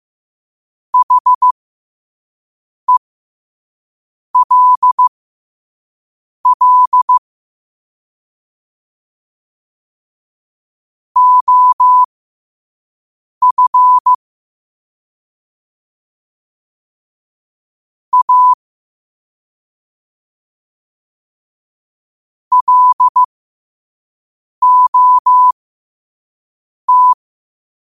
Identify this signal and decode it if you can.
Morse code, heard as a keyed tone, about 15 wpm, spaced as for 5 wpm, 1000 Hz. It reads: HELL OF A LOT
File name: qotdfiles / 5wpm